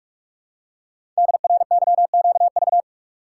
Morse code, heard as a keyed tone, about 44 wpm, 700 Hz. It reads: DRYQV
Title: Code Groups - 5 Letters - 45wpm